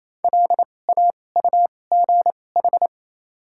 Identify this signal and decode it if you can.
Morse code, heard as a keyed tone, about 28 wpm, spaced as for 20 wpm, 700 Hz. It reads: LAUGH
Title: 2x Character Spacing - Top 500 Words - 28wpm